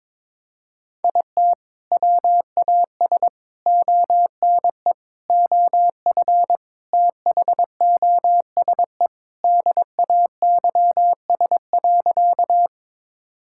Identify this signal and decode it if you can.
Morse code, heard as a keyed tone, about 22 wpm, 700 Hz. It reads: IT WAS ONE OF THOSE DAYS.
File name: practice / 22wpm / Sentences with 4-Word Phrases - 22wpm